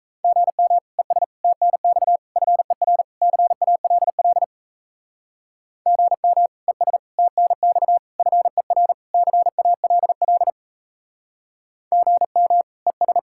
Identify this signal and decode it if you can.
Morse code, heard as a keyed tone, about 38 wpm, 700 Hz. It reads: GM ES TNX FER CALL GM ES TNX FER CALL GM ES